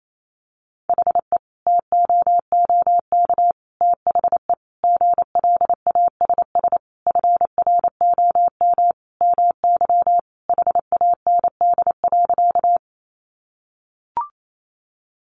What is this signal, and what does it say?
Morse code, heard as a keyed tone, about 28 wpm, 700 Hz. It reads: HE TOOK THE GLASS FROM MY HAND.
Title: Sentences from Top 1300 Words 28wpm